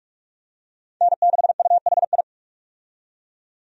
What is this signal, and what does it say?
Morse code, heard as a keyed tone, about 44 wpm, 700 Hz. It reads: N6UHI
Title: USA Call Signs - 5 Characters - 45wpm